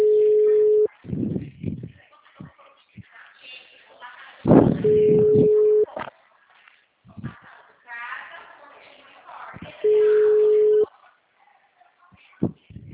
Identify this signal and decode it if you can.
Ringback tone (Europe, 425 Hz), with speech.
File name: out- -2012- - - .34890